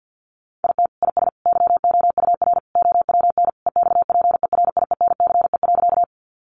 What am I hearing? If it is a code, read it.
Morse code, heard as a keyed tone, about 50 wpm, 700 Hz. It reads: IT IS YOUR OWN EXPERIENCE.
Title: Sentences from Top 600 Words 50wpm